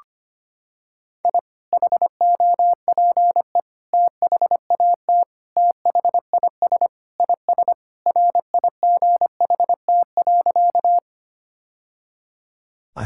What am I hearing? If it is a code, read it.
Morse code, heard as a keyed tone, about 25 wpm, 700 Hz. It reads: I HOPE THAT THIS IS RIGHT.
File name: practice / 25wpm / Sentences with 3-Word Phrases - 25wpm